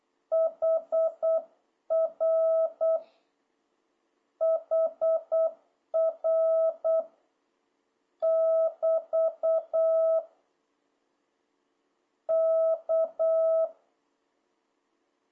Morse code, heard as a keyed tone, about 8 wpm, 1300 Hz, with noise. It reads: HR HR <BT> K
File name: jh8cbh / 2026cw